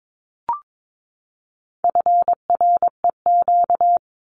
Morse code, heard as a keyed tone, about 22 wpm, 700 Hz. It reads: FREQ
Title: Abbreviations 22wpm